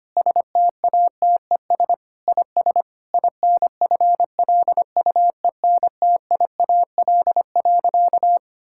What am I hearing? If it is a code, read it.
Morse code, heard as a keyed tone, about 25 wpm, 700 Hz. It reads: STATES IS INFLUENTIAL.
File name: practice / 25wpm / Sentences with 4-Word Phrases - 25wpm